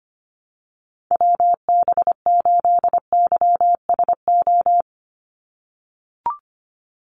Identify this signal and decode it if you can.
Morse code, heard as a keyed tone, about 25 wpm, 700 Hz. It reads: WB8YSO
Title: USA Call Signs 25wpm